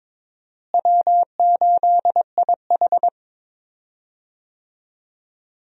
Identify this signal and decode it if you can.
Morse code, heard as a keyed tone, about 22 wpm, 700 Hz. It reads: W8IH
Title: USA Call Signs - 4 Characters - Repeated 2x - 22wpm